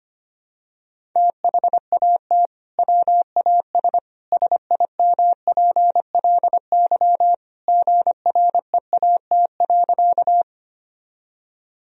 Morse code, heard as a keyed tone, about 25 wpm, 700 Hz. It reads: THAT WAS SIMPLY GREAT.
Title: Sentences from Top 1000 Words - Review Part 4 - 25wpm